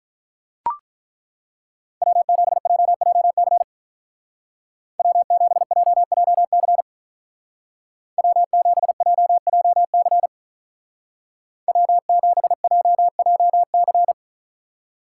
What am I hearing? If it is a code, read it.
Morse code, heard as a keyed tone, about 42 wpm, 700 Hz. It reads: W7JJC W7JJC W7JJC W7JJC
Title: Speed Racing - USA 1x3 Call Signs 35wpm